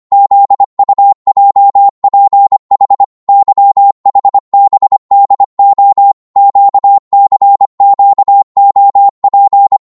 Morse code, heard as a keyed tone, about 25 wpm, 800 Hz. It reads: ZUJPH YHBDO QCQOP
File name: Morse / ltr-25wpm